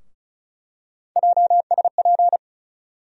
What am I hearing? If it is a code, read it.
Morse code, heard as a keyed tone, about 36 wpm, 700 Hz. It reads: JSP
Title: USA Call Sign Postfixes - 3 Letters 35wpm